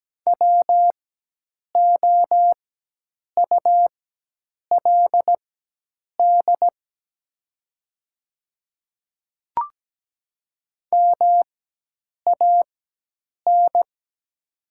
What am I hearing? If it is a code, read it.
Morse code, heard as a keyed tone, about 17 wpm, spaced as for 8 wpm, 700 Hz. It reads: WOULD MAN
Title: 4x Character Spacing - Top 300 Words - 17wpm